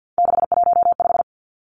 Morse code, heard as a keyed tone, about 50 wpm, 700 Hz. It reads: 615
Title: Three Challenging Letter-Number Combinations 50wpm